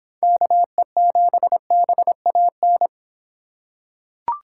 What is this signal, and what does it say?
Morse code, heard as a keyed tone, about 26 wpm, 700 Hz. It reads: KE7BAN